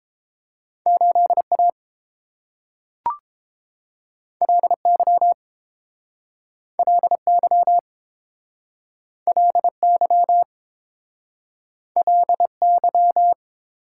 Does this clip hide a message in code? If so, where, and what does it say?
Yes: Morse code from 0.9 to 13.3 s, heard as a keyed tone, about 30 wpm, 700 Hz. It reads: 8 A L Y L Y L Y L Y